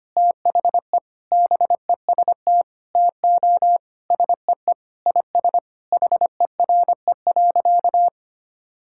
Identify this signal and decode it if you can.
Morse code, heard as a keyed tone, about 25 wpm, 700 Hz. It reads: THE BEST TO SEE IS HERE.
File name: practice / 25wpm / Sentences with 4-Word Phrases - 25wpm